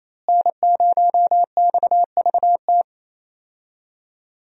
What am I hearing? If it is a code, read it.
Morse code, heard as a keyed tone, about 28 wpm, 700 Hz. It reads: N0XVT
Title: USA Call Signs - 5 Characters - Repeated 2x - 28wpm